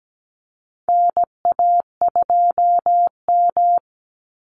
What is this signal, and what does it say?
Morse code, heard as a keyed tone, about 17 wpm, 700 Hz. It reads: NA2M
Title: Call Signs Repeated 2x - 17wpm